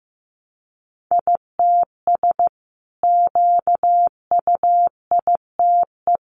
Morse code, heard as a keyed tone, about 15 wpm, 700 Hz. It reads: ITS QUITE
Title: Sentences from Top 700 Words 15wpm